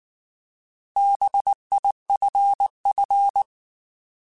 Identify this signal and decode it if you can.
Morse code, heard as a keyed tone, about 19 wpm, 800 Hz. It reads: BIFF
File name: Alphabetical / Alphabetical mp3s / Bs @18